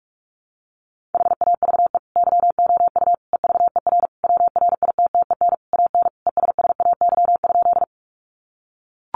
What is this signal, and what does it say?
Morse code, heard as a keyed tone, about 44 wpm, 700 Hz. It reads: HAVE YOU EVER WRITTEN AN ESSAY?
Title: Sentences from Top 1500 Words 45wpm